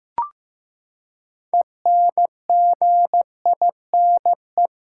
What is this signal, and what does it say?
Morse code, heard as a keyed tone, about 15 wpm, 700 Hz. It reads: ENGINE
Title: Six Letter Words - 15wpm